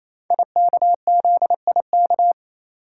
Morse code, heard as a keyed tone, about 28 wpm, 690 Hz. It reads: IKZIK
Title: Code Groups - 5 Letters - 28wpm